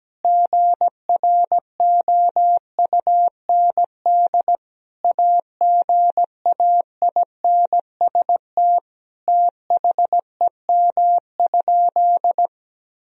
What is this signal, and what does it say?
Morse code, heard as a keyed tone, about 17 wpm, 700 Hz. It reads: GROUND AGAINST THEM?
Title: Sentences from Top 1000 Words - Review Part 2 - 17wpm